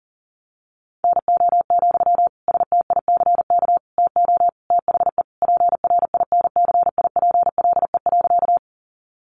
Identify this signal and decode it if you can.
Morse code, heard as a keyed tone, about 40 wpm, 700 Hz. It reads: NO, STICK TO THE PRINCIPLE.
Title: Sentences from Top 1400 Words 40wpm